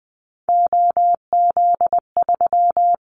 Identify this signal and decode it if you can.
Morse code, heard as a keyed tone, about 20 wpm, 700 Hz. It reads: OZ3
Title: Callsign Prefixes for Top 100 DXCC - 20wpm